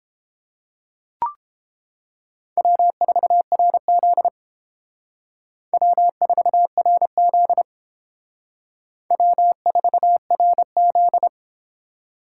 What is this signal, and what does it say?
Morse code, heard as a keyed tone, about 32 wpm, 700 Hz. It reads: W4RZ W4RZ W4RZ